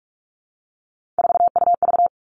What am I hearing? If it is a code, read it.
Morse code, heard as a keyed tone, about 46 wpm, 700 Hz. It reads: 4UV
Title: Three Challenging Letter-Number Combinations 45wpm